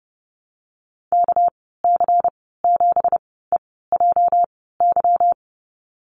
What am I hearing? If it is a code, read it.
Morse code, heard as a keyed tone, about 30 wpm, spaced as for 17 wpm, 700 Hz. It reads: KC7EJY
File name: Lesson 37g - Letter x - Call Signs 30wpm